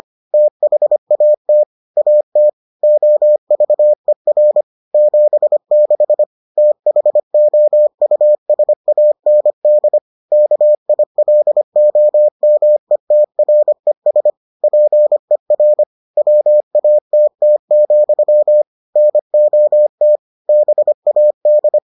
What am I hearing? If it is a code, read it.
Morse code, heard as a keyed tone, about 25 wpm, 600 Hz. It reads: THAT AT OVER 76 THOUSAND KILOMETRES PER WATT, NOT BAD